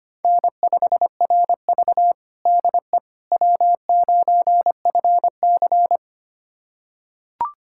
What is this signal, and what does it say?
Morse code, heard as a keyed tone, about 25 wpm, 700 Hz. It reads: N5RV DE W9FC
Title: USA Callsign DE Callsign - 4 Characters - 25wpm